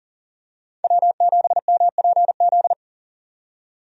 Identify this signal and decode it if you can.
Morse code, heard as a keyed tone, about 37 wpm, 700 Hz. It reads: W7MPZ
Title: USA Call Signs - 5 Characters - 40wpm